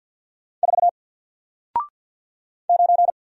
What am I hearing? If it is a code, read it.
Morse code, heard as a keyed tone, about 54 wpm, 700 Hz. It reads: 4 9
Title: Single Letter-Number Alt 50wpm